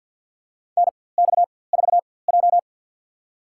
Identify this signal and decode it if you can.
Morse code, heard as a keyed tone, about 48 wpm, spaced as for 23 wpm, 690 Hz. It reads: NX4J
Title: USA Call Signs - 4x Character Spacing - 50wpm